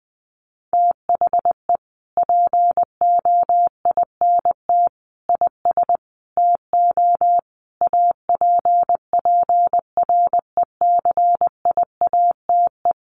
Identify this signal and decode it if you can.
Morse code, heard as a keyed tone, about 20 wpm, 700 Hz. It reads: THE POINT IS TO APPRECIATE
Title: Sentences from Top 1300 Words 20wpm